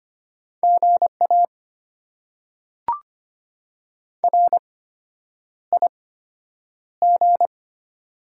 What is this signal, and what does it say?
Morse code, heard as a keyed tone, about 25 wpm, spaced as for 9 wpm, 700 Hz. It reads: GA RIG
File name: Top 90 QSO Elements - 8x Character Spacing - 25wpm